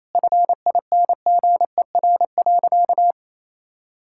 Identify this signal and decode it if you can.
Morse code, heard as a keyed tone, about 28 wpm, 700 Hz. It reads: FINGER.
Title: Sentences from Top 1600 Words 28wpm